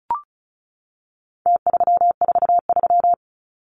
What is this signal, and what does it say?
Morse code, heard as a keyed tone, about 34 wpm, 700 Hz. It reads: T343
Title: Numbers - Contest Style - 4-Digit 35wpm